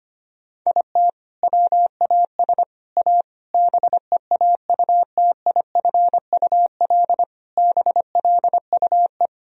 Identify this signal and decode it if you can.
Morse code, heard as a keyed tone, about 25 wpm, 700 Hz. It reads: IT WAS A BEAUTIFUL BLUE
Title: Sentences from Top 1000 Words - Review Part 3 - 25wpm